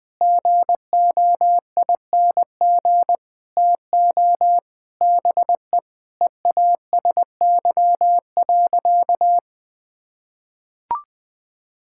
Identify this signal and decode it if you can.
Morse code, heard as a keyed tone, about 20 wpm, 700 Hz. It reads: GOING TO BE EASY.
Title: Sentences with 5-Word Phrases - Spoken Beginning - 20wpm